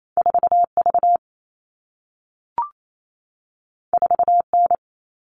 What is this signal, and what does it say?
Morse code, heard as a keyed tone, about 28 wpm, 700 Hz. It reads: 4V 4N
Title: All Cut Numbers - 2-Digit 28wpm